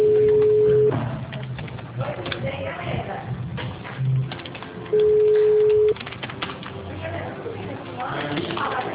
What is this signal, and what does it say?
Ringback tone (Europe, 425 Hz), with speech.